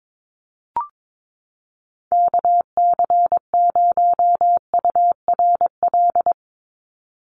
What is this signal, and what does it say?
Morse code, heard as a keyed tone, about 22 wpm, 700 Hz. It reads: KC0URL
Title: Call Signs - 22wpm